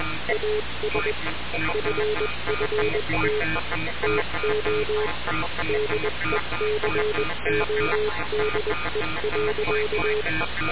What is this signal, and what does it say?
Morse code, heard as a keyed tone, about 21 wpm, 420 Hz, with noise. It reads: AS FFT TO DECODE.
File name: radiocap / REC z